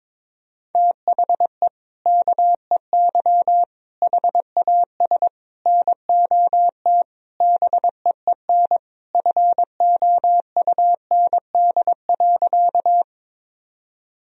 Morse code, heard as a keyed tone, about 21 wpm, 700 Hz. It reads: THE KEY HAS NOT BEEN FOUND.
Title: Sentences from Top 1000 Words - Review Part 4 - 22wpm